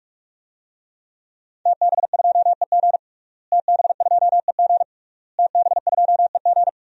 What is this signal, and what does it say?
Morse code, heard as a keyed tone, about 46 wpm, 700 Hz. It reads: TB2EG TB2EG TB2EG
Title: DXCC-100 Call Signs - 5 Characters - Repeated 3x - 45wpm